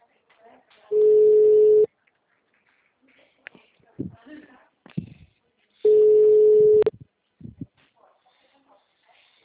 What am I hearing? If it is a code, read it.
Ringback tone (Europe, 425 Hz), with speech.